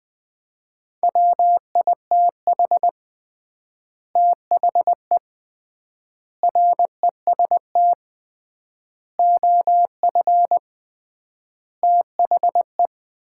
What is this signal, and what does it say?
Morse code, heard as a keyed tone, about 20 wpm, 700 Hz. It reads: WITH THE REST OF THE
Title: 3x Word-Spacing - Sets of 5 Words - 20wpm